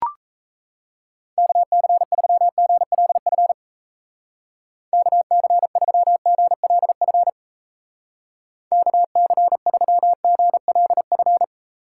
Morse code, heard as a keyed tone, about 37 wpm, 700 Hz. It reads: KC3GLF KC3GLF KC3GLF